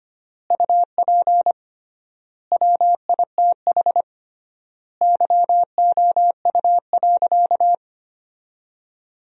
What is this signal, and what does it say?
Morse code, heard as a keyed tone, about 25 wpm, 700 Hz. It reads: UP WITH YOU.
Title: 3x Word-Spacing - Sentences from Top 300 Words - 25wpm